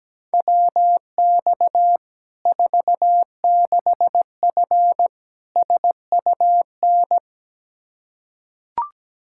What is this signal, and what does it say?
Morse code, heard as a keyed tone, about 17 wpm, 700 Hz. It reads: WX 46F SUN
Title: Practice QSO - Exchange 2 - Info - 17wpm